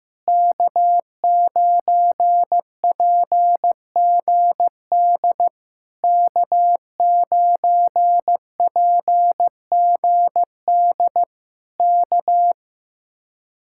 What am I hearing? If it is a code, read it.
Morse code, heard as a keyed tone, about 15 wpm, 700 Hz. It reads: K9PGD K9PGD K
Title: Speed-Racing Common QSO Phrases Encore - 15wpm